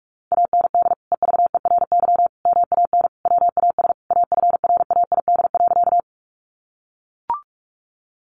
Morse code, heard as a keyed tone, about 44 wpm, 700 Hz. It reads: AND EVERY MAN WAS AFRAID.
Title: Sentences from Top 1500 Words 45wpm